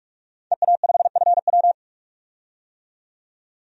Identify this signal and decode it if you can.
Morse code, heard as a keyed tone, about 43 wpm, 700 Hz. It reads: EA5FW
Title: DXCC-100 Call Signs - 5 Characters - 45wpm